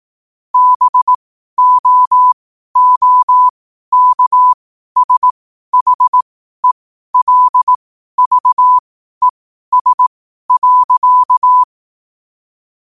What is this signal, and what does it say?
Morse code, heard as a keyed tone, about 18 wpm, spaced as for 12 wpm, 1000 Hz. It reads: BOOKSHELVES.
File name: qotdfiles / 12wpm